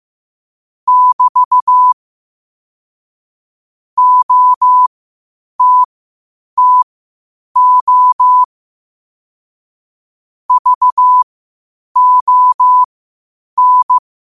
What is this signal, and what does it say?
Morse code, heard as a keyed tone, about 15 wpm, spaced as for 8 wpm, 1000 Hz. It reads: <BT> OTTO VON
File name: qotdfiles / 8wpm